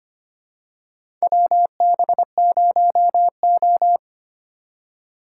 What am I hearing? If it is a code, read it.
Morse code, heard as a keyed tone, about 25 wpm, 700 Hz. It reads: WB0O
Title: USA Call Signs - 4 Characters - 4x Character Spacing - 25wpm